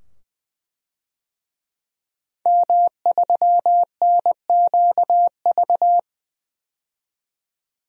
Morse code, heard as a keyed tone, about 20 wpm, 700 Hz. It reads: M3NQV
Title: DXCC-100 Call Signs - 5 Characters - 20wpm